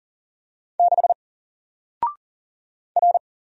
Morse code, heard as a keyed tone, about 40 wpm, 700 Hz. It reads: 6 R
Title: Single Character (Letter:Number:Punctuation:Prosign) 40wpm